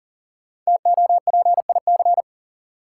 Morse code, heard as a keyed tone, about 40 wpm, 700 Hz. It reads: TOPIC
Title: Sentences from Top 1200 Words - New Word Drill Alt - 40wpm